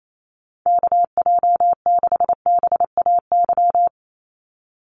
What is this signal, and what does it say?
Morse code, heard as a keyed tone, about 28 wpm, 700 Hz. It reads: KJ6BAY
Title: USA Call Signs 28wpm